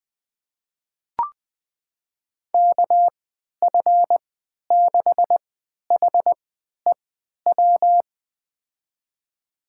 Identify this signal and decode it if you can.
Morse code, heard as a keyed tone, about 20 wpm, spaced as for 11 wpm, 700 Hz. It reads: KF6HEW
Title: USA Call Signs - 6 Characters - 3x Character Spacing - 20wpm